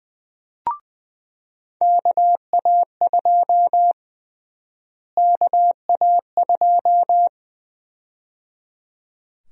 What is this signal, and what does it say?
Morse code, heard as a keyed tone, about 20 wpm, 700 Hz. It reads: KA2 KA2
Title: USA Call Sign Prefixes - Repeated 2x - 20wpm